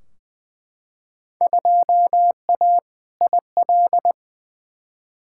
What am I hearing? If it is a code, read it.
Morse code, heard as a keyed tone, about 20 wpm, 700 Hz. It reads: 2A IL
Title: Field Day Exchanges - 20wpm